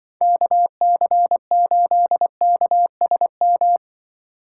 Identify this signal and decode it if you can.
Morse code, heard as a keyed tone, about 24 wpm, 700 Hz. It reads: KC8KSM
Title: Speed Racing - USA Call Signs 20wpm